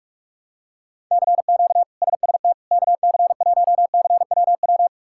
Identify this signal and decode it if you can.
Morse code, heard as a keyed tone, about 44 wpm, 700 Hz. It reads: CQ SST KC1CWW
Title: K1USN SST QSO Practice - 45wpm